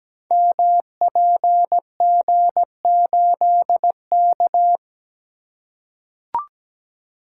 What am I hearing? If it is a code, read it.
Morse code, heard as a keyed tone, about 17 wpm, 700 Hz. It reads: MPG8K
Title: Code Groups - 5 Letter:Numbers - 17wpm